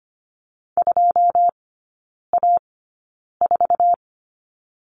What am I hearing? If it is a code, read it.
Morse code, heard as a keyed tone, about 25 wpm, spaced as for 9 wpm, 700 Hz. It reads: 2A4
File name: Lesson 17e - Letter u - Three Characters 25wpm